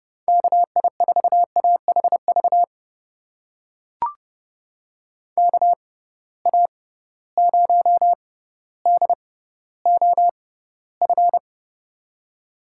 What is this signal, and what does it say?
Morse code, heard as a keyed tone, about 32 wpm, spaced as for 16 wpm, 700 Hz. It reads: KI4AHV KA0DOF
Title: USA Call Signs - 6 Characters - 6x Character Spacing - 30wpm